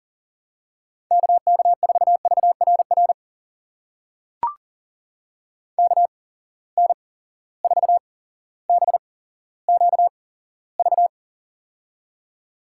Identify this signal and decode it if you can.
Morse code, heard as a keyed tone, about 37 wpm, spaced as for 17 wpm, 700 Hz. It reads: KK4VRR KN4BQV
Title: USA Call Signs - 6 Characters - 8x Character Spacing - 40wpm